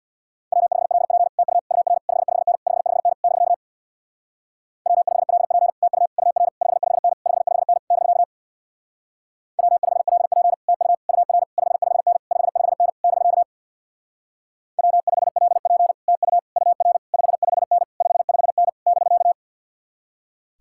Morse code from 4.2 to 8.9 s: W5LP TU UR 55N 55N <BK>

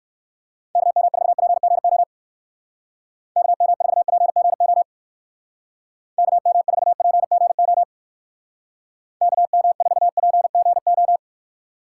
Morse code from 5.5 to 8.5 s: KM4PGO

